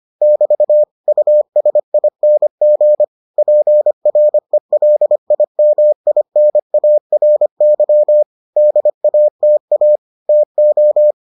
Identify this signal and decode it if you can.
Morse code, heard as a keyed tone, about 25 wpm, 600 Hz. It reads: <BT> USING PRELIMINARY DATA TO